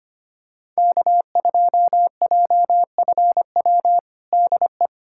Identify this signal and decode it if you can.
Morse code, heard as a keyed tone, about 25 wpm, 700 Hz. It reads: K2JFW DE
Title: USA Callsign DE Callsign - 5 Characters - 25wpm